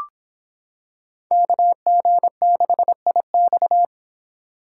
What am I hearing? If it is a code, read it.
Morse code, heard as a keyed tone, about 25 wpm, 700 Hz. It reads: KG6IX